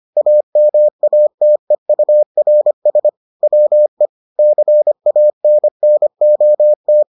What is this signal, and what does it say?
Morse code, heard as a keyed tone, about 25 wpm, 600 Hz. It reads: AMATEURS WE CANNOT